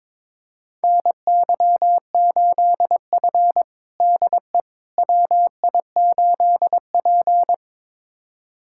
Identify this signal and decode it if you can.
Morse code, heard as a keyed tone, about 22 wpm, 700 Hz. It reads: NY8F DE WI8P
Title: USA Callsign DE Callsign - 4 Characters - 22wpm